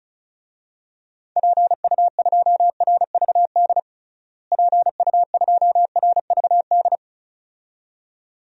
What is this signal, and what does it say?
Morse code, heard as a keyed tone, about 34 wpm, 700 Hz. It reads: PU2RVD PU2RVD
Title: DXCC-100 Call Signs - 6 Characters - Repeated 2x - 35wpm